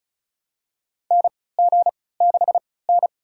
Morse code, heard as a keyed tone, about 34 wpm, spaced as for 20 wpm, 700 Hz. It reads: NG6N
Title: USA Call Signs - 4 Characters - 3x Character Spacing - 35wpm